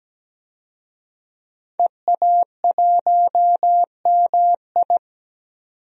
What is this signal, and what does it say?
Morse code, heard as a keyed tone, about 17 wpm, 700 Hz. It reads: EA1MI
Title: DXCC-100 Call Signs - 5 Characters - 17wpm